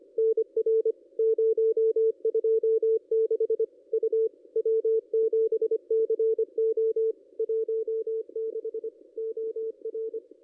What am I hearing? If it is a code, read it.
Morse code, heard as a keyed tone, about 25 wpm, 440 Hz. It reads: NR 026 U W7CO 16 OR
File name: contests / ss-cw-k5tr-op-n5rz -2200